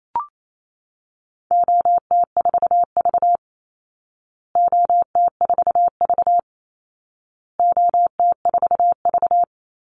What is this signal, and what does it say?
Morse code, heard as a keyed tone, about 28 wpm, 700 Hz. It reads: OT4V OT4V OT4V
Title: Call Signs Repeated 2x - 28wpm